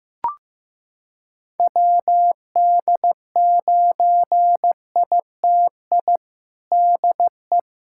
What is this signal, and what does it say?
Morse code, heard as a keyed tone, about 15 wpm, 700 Hz. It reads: WD9ITI DE
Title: USA Callsign DE Callsign - 6 Characters - 15wpm